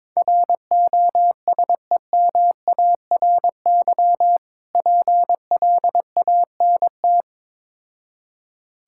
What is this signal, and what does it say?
Morse code, heard as a keyed tone, about 22 wpm, 690 Hz. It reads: ROSEMARY PLANT
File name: Difficult Words with Hints - 22wpm